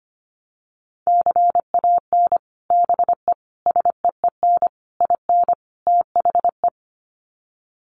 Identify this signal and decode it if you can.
Morse code, heard as a keyed tone, about 25 wpm, 700 Hz. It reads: CAN BE SEEN IN THE
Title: Sets of 5 Words Encore2 25wpm